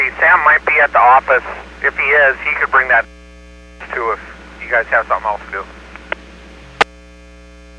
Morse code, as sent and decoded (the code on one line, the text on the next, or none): none